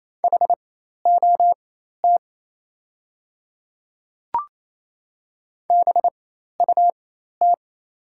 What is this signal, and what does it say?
Morse code, heard as a keyed tone, about 28 wpm, spaced as for 13 wpm, 700 Hz. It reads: HOT BUT
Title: 4x Character Spacing - Top 100 Words - 28wpm